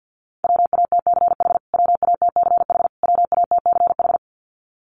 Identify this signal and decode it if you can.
Morse code, heard as a keyed tone, about 48 wpm, 700 Hz. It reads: WATCH WATCH WATCH
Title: Top 500 Words Repeated 50wpm